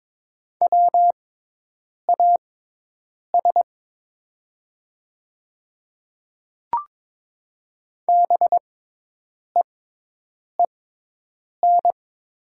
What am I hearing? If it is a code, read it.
Morse code, heard as a keyed tone, about 22 wpm, spaced as for 8 wpm, 700 Hz. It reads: WAS BEEN
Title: 6x Character Spacing - Top 300 Words - 22wpm